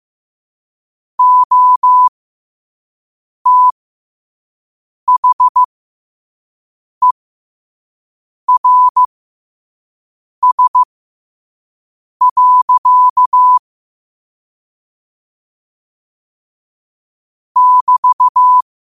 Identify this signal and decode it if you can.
Morse code, heard as a keyed tone, about 15 wpm, spaced as for 5 wpm, 1000 Hz. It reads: OTHERS. <BT>
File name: qotdfiles / 5wpm